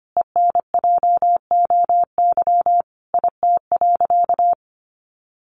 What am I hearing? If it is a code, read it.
Morse code, heard as a keyed tone, about 25 wpm, 700 Hz. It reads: ENJOY IT.
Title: Sentences from Top 1400 Words 25wpm